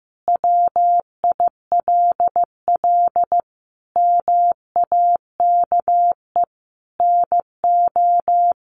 Morse code, heard as a keyed tone, about 14 wpm, 700 Hz. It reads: WILL MAKE NO
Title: Sentences from Top 800 Words 15wpm